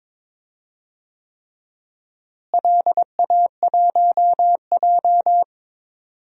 Morse code, heard as a keyed tone, about 22 wpm, 700 Hz. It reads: LA1J
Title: DXCC-100 Call Signs - 4 Characters - Repeated 3x - 22wpm